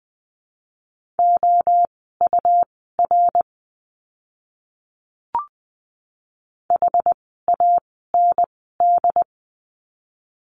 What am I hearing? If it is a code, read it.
Morse code, heard as a keyed tone, about 20 wpm, spaced as for 14 wpm, 700 Hz. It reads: OUR HAND
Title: 115 Lesson 19f - Letter w - Words 20wpm